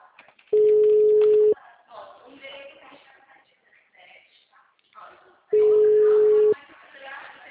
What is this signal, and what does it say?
Ringback tone (Europe, 425 Hz), with speech.